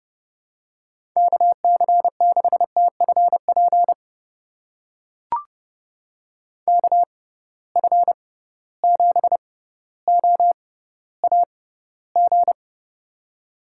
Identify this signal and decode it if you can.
Morse code, heard as a keyed tone, about 30 wpm, spaced as for 15 wpm, 700 Hz. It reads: KC6TFP KF7OAG
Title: USA Call Signs - 6 Characters - 6x Character Spacing - 30wpm